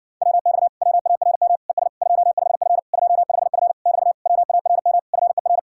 Morse code, heard as a keyed tone, about 52 wpm, 700 Hz. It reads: WX WARM ES 25F 25F <BT> WARM FER